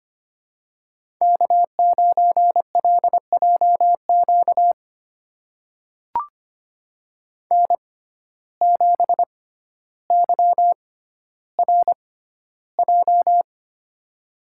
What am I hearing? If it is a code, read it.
Morse code, heard as a keyed tone, about 25 wpm, spaced as for 13 wpm, 700 Hz. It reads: K9LJQ N7YRJ
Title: USA Call Signs - 5 Characters - 6x Character Spacing - 25wpm